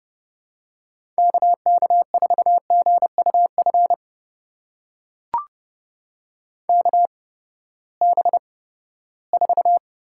Morse code, heard as a keyed tone, about 30 wpm, spaced as for 17 wpm, 700 Hz. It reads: KK4GUF KB4
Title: USA Call Signs - 6 Characters - 8x Character Spacing - 30wpm